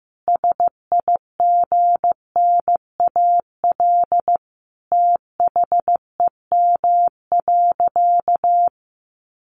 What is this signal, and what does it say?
Morse code, heard as a keyed tone, about 15 wpm, 700 Hz. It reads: SIGNAL THEM.
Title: Sentences from Top 1400 Words 15wpm